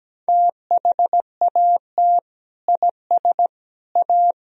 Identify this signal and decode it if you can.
Morse code, heard as a keyed tone, about 17 wpm, 700 Hz. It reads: THAT IS A
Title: Sentences from Top 1000 Words - Review Part 4 - 17wpm